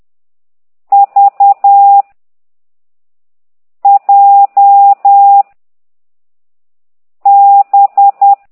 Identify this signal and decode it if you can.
Morse code, heard as a keyed tone, about 10 wpm, 800 Hz. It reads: V J B